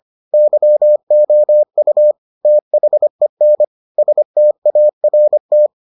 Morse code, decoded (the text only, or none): YOU THEN START